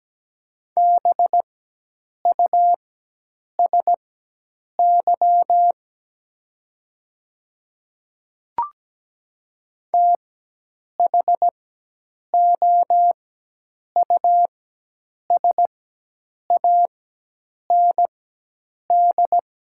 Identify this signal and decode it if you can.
Morse code, heard as a keyed tone, about 17 wpm, spaced as for 8 wpm, 690 Hz. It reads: BUSY THOUSAND
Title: 4x Character Spacing - Top 500 Words - 17wpm